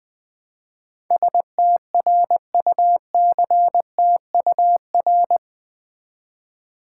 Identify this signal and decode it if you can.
Morse code, heard as a keyed tone, about 20 wpm, 700 Hz. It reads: STRUCTUR
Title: Build A Word - 4 Letters At A Time - 10-Letter Words - 20wpm